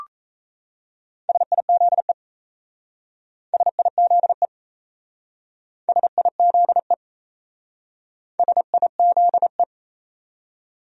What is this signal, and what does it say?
Morse code, heard as a keyed tone, about 35 wpm, 700 Hz. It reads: SIZE SIZE SIZE SIZE
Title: Speed-Racing Top 500 Words 28wpm